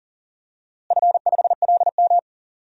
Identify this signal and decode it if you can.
Morse code, heard as a keyed tone, about 40 wpm, 700 Hz. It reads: F5LM
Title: DXCC-100 Call Signs - 4 Characters - Repeated 3x - 40wpm